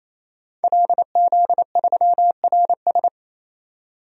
Morse code, heard as a keyed tone, about 28 wpm, 690 Hz. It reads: LZ3RS
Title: DXCC-100 Call Signs - 5 Characters - 28wpm